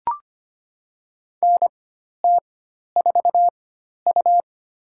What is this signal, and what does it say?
Morse code, heard as a keyed tone, about 25 wpm, spaced as for 12 wpm, 700 Hz. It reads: NT4U
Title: USA Call Signs - 4 Characters - 4x Character Spacing - 25wpm